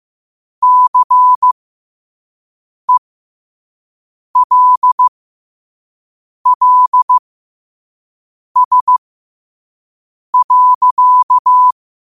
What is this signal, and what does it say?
Morse code, heard as a keyed tone, about 15 wpm, spaced as for 5 wpm, 1000 Hz. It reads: CELLS.
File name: qotdfiles / 5wpm